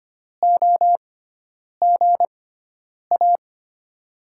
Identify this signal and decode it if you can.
Morse code, heard as a keyed tone, about 25 wpm, spaced as for 9 wpm, 700 Hz. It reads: OGA
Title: USA Call Signs - 5 Characters - 6x Character Spacing - 25wpm